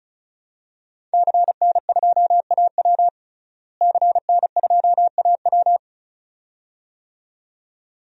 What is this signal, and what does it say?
Morse code, heard as a keyed tone, about 36 wpm, 700 Hz. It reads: CN2AW CN2AW
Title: DXCC-100 Call Signs - 5 Characters - Repeated 2x - 35wpm